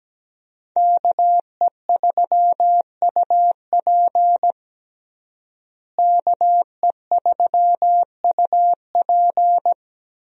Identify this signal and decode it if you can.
Morse code, heard as a keyed tone, about 17 wpm, 700 Hz. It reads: KE3UP KE3UP
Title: USA Call Signs - 5 Characters - Repeated 3x - 17wpm